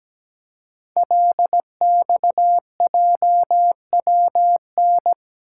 Morse code, heard as a keyed tone, about 17 wpm, 700 Hz. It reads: LXJWN
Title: Code Groups - 5 Letter:Numbers - 17wpm